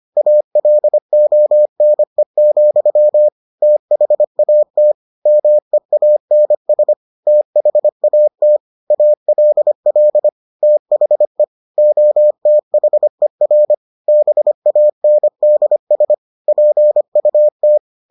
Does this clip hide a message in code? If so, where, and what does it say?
Yes: Morse code from 0.2 to 17.8 s, heard as a keyed tone, about 25 wpm, 600 Hz. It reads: ALONE, THAT MEANS THAT ALL THE OTHER BANDS PUT